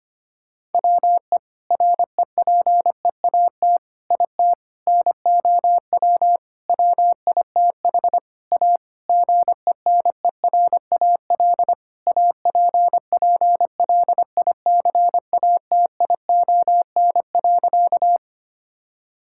Morse code, heard as a keyed tone, about 25 wpm, 700 Hz. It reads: WE REPEAT IT NOW WITH A GENERAL APPLICATION.
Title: Sentences from Top 1600 Words 25wpm